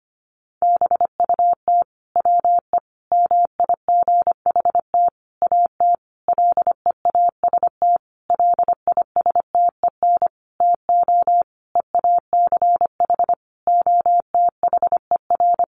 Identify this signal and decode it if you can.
Morse code, heard as a keyed tone, about 25 wpm, 700 Hz. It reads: BUT WE MIGHT AT LEAST LISTEN TO EACH OTHER
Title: Sentences from Top 1300 Words 25wpm